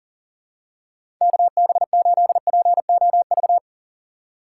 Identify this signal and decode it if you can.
Morse code, heard as a keyed tone, about 37 wpm, 700 Hz. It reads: KB8POV
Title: USA Call Signs - 6 Characters - 4x Character Spacing - 40wpm